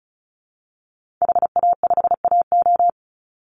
Morse code, heard as a keyed tone, about 34 wpm, 700 Hz. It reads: HA5AO
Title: Call Signs - 35wpm